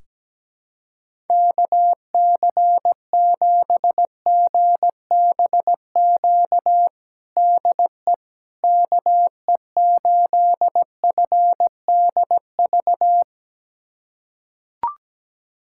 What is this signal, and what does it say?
Morse code, heard as a keyed tone, about 17 wpm, 700 Hz. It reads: KC7GBQ DE KE8FDV
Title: USA Callsign DE Callsign - 6 Characters - 17wpm